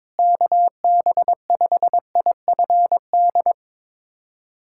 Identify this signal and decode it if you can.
Morse code, heard as a keyed tone, about 22 wpm, 700 Hz. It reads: KB5IFD
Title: USA Callsign DE Callsign 22wpm